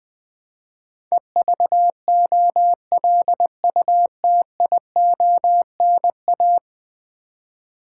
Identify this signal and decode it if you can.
Morse code, heard as a keyed tone, about 20 wpm, 700 Hz. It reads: EVOLUTIONA
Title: Build A Word - 5 Letters At A Time - 12-Letter Words - 20wpm